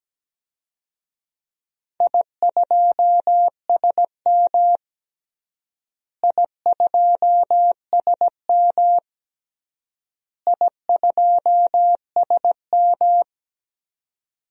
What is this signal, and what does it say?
Morse code, heard as a keyed tone, about 17 wpm, 700 Hz. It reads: I2SM I2SM I2SM
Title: DXCC-100 Call Signs - 4 Characters - Repeated 3x - 17wpm